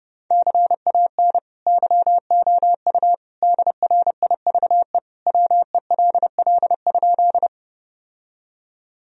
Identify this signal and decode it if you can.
Morse code, heard as a keyed tone, about 32 wpm, 700 Hz. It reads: CAN YOU DRIVE WELL?
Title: Sentences from Top 1000 Words - Review Part 2 - 30wpm